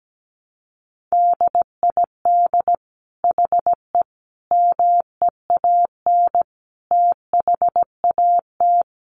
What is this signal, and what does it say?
Morse code, heard as a keyed tone, about 17 wpm, 700 Hz. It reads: DID HE MEAN THAT
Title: Sentences from Top 1400 Words 17wpm